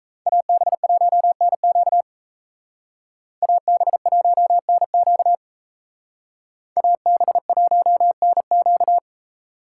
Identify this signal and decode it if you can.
Morse code, heard as a keyed tone, about 33 wpm, 700 Hz. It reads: AB1NQ AB1NQ AB1NQ
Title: Speed Racing - USA 2x2 Call Signs 28wpm